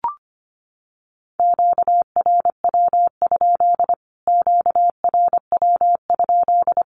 Morse code, heard as a keyed tone, about 25 wpm, 700 Hz. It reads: QRW? QRW?
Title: Q-Codes Repeated 3x 25wpm